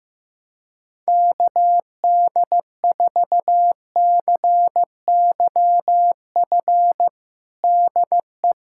Morse code, heard as a keyed tone, about 15 wpm, 700 Hz. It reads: KD4CYF DE
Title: USA Callsign DE Callsign - 6 Characters - 15wpm